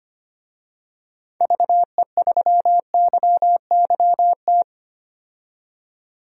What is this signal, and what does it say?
Morse code, heard as a keyed tone, about 25 wpm, 700 Hz. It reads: VE3YYT
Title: DXCC-100 Call Signs - 6 Characters - 25wpm